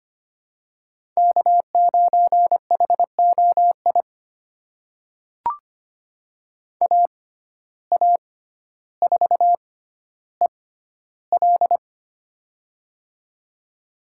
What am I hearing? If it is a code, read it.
Morse code, heard as a keyed tone, about 25 wpm, spaced as for 13 wpm, 700 Hz. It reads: K9HOI AA4EL
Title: USA Call Signs - 5 Characters - 6x Character Spacing - 25wpm